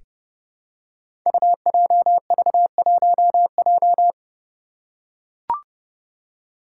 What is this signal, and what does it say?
Morse code, heard as a keyed tone, about 28 wpm, 700 Hz. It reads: UJV1J
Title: Five Challenging Letter-Number Combinations 30wpm